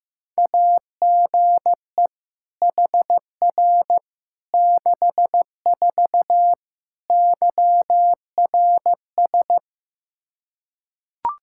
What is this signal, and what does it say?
Morse code, heard as a keyed tone, about 15 wpm, 700 Hz. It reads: AGE HR 64 YRS
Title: Practice QSO - Exchange 3 - More Info - 15wpm